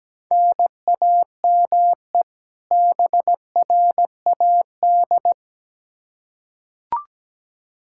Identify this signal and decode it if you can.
Morse code, heard as a keyed tone, about 17 wpm, 700 Hz. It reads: NAME BRAD
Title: Practice QSO - Exchange 1 - Beginning - 17wpm